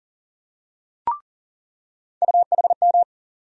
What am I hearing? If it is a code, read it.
Morse code, heard as a keyed tone, about 40 wpm, 700 Hz. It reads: UHM
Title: USA Call Sign Postfixes - 3 Letters 40wpm